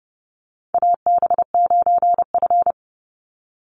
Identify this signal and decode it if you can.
Morse code, heard as a keyed tone, about 30 wpm, 700 Hz. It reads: AB9F
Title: USA Call Signs 30wpm